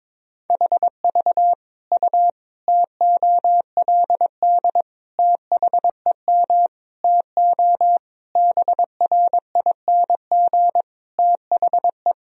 Morse code, heard as a keyed tone, about 22 wpm, 700 Hz. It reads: HV U TOLD THEM TO BRING THE
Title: Sentences from Top 500 Words and Abbreviations 22wpm